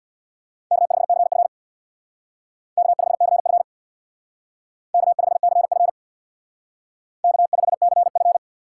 Morse code, heard as a keyed tone, about 60 wpm, 690 Hz. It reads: K5CF K5CF K5CF K5CF
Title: Speed Racing - USA 1x2 Call Signs 50wpm